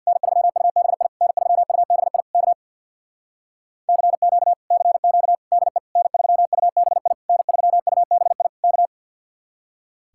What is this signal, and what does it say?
Morse code, heard as a keyed tone, about 61 wpm, 700 Hz. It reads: N3UBI N3UBI K CQ CQ DE N3UBI N3UBI K
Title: Speed-Racing Common QSO Phrases Encore - 50wpm